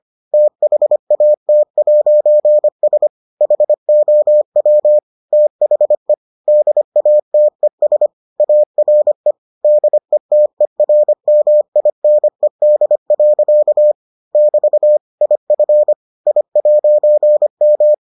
Morse code, heard as a keyed tone, about 25 wpm, 600 Hz. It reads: THAT'S HOW THE DATES ARE DETERMINED. <BT> IF I'M